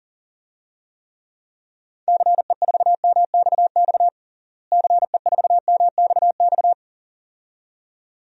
Morse code, heard as a keyed tone, about 37 wpm, 700 Hz. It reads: CE4MXX CE4MXX
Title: DXCC-100 Call Signs - 6 Characters - Repeated 2x - 40wpm